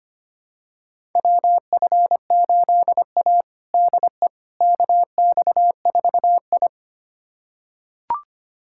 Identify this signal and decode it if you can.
Morse code, heard as a keyed tone, about 25 wpm, 700 Hz. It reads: WF8A DE KX4I
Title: USA Callsign DE Callsign - 4 Characters - 25wpm